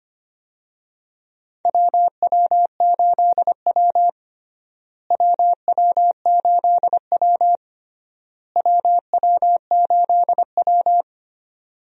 Morse code, heard as a keyed tone, about 25 wpm, 700 Hz. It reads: WW8W WW8W WW8W
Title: USA Call Signs - 4 Characters - Repeated 3x - 25wpm